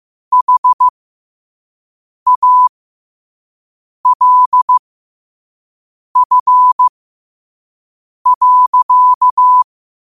Morse code, heard as a keyed tone, about 15 wpm, spaced as for 5 wpm, 1000 Hz. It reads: HALF.
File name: qotdfiles / 5wpm